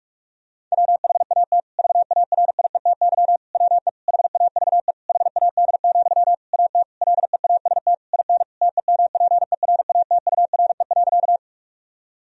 Morse code, heard as a keyed tone, about 46 wpm, 700 Hz. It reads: WHAT VARIETY WE HAVE HAD, AT LEAST IN TEMPERATURE.
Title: Sentences from Top 1000 Words - Review Part 4 - 45wpm